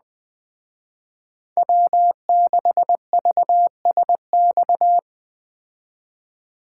Morse code, heard as a keyed tone, about 20 wpm, 700 Hz. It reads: W6VSX
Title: Speed Racing - USA 1x3 Call Signs 17wpm